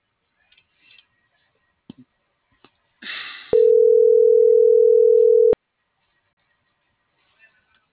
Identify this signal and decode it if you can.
Ringback tone (North America), 440 + 480 Hz.